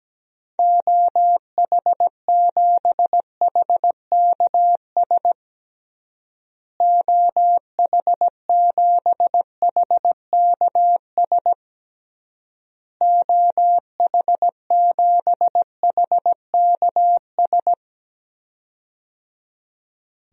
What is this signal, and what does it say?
Morse code, heard as a keyed tone, about 17 wpm, 700 Hz. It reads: OH7HKS OH7HKS OH7HKS
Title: DXCC-100 Call Signs - 6 Characters - Repeated 3x - 17wpm